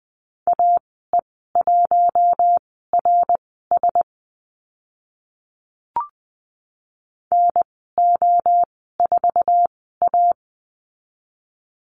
Morse code, heard as a keyed tone, about 20 wpm, spaced as for 14 wpm, 700 Hz. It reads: AE1RS NO4A
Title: 055 Lesson 10g - Letter r - Call Signs 20wpm